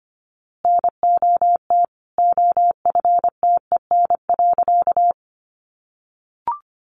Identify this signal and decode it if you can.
Morse code, heard as a keyed tone, about 25 wpm, 700 Hz. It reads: NOT OFTEN.
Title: A Sound Of Thunder 25wpm